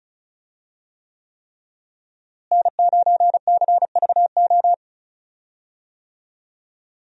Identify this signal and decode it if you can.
Morse code, heard as a keyed tone, about 34 wpm, 700 Hz. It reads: N9CVO
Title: USA Call Signs - 5 Characters - Repeated 3x - 35wpm